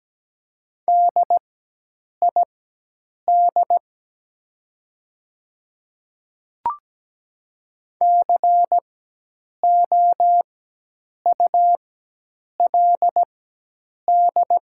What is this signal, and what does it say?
Morse code, heard as a keyed tone, about 17 wpm, spaced as for 8 wpm, 690 Hz. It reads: DID COULD
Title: 4x Character Spacing - Top 200 Words - 17wpm